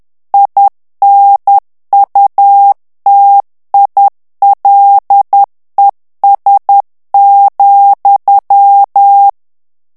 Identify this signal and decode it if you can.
Morse code, heard as a keyed tone, about 10 wpm, 800 Hz. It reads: INUTILES,
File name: FRANCAIS TEXTE CLAIR 2 VITESSE 10 MOTS MINUTES